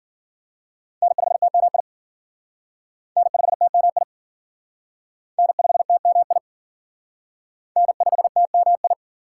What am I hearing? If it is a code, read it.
Morse code, heard as a keyed tone, about 51 wpm, 700 Hz. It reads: N5TMI N5TMI N5TMI N5TMI